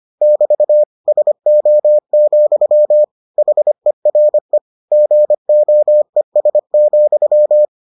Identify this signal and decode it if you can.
Morse code, heard as a keyed tone, about 25 wpm, 600 Hz. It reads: <BT> SO, HERE GOES,